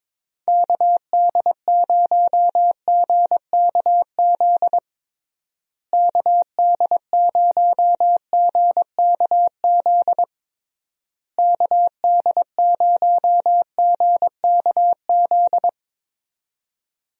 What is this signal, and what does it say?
Morse code, heard as a keyed tone, about 22 wpm, 700 Hz. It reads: KD0GKZ KD0GKZ KD0GKZ
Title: USA Call Signs - 6 Characters - Repeated 3x - 22wpm